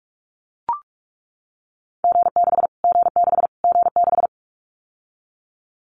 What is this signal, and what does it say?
Morse code, heard as a keyed tone, about 44 wpm, 700 Hz. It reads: G6 G6 G6
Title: Callsign Prefixes for Top 25 DXCC - Repeated 3x - 45wpm